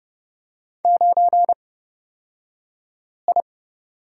Morse code, heard as a keyed tone, about 29 wpm, spaced as for 5 wpm, 690 Hz. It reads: 9I
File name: Lesson 36d - BT - Two Characters 30wpm